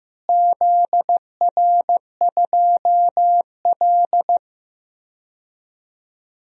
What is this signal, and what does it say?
Morse code, heard as a keyed tone, about 15 wpm, 700 Hz. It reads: ZR2L
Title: DXCC-100 Call Signs - 4 Characters - Repeated 3x - 15wpm